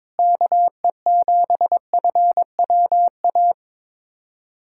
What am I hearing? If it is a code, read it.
Morse code, heard as a keyed tone, about 22 wpm, 700 Hz. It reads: KE7FWA
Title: USA Callsign DE Callsign 22wpm